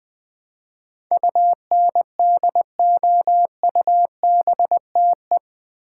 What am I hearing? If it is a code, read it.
Morse code, heard as a keyed tone, about 20 wpm, 700 Hz. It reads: UNDOUBTE
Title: Build A Word - 4 Letters At A Time - 11-Letter Words - 20wpm